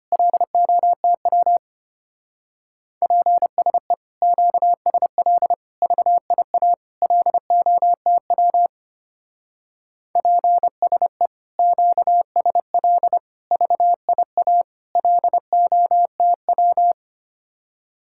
Morse code, heard as a keyed tone, about 32 wpm, 700 Hz. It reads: LOTW PSE QSL VIA LOTW PSE QSL VIA LOTW